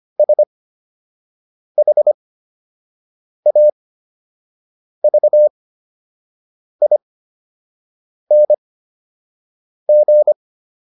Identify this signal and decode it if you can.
Morse code, heard as a keyed tone, about 25 wpm, spaced as for 6 wpm, 600 Hz. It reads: SHAVING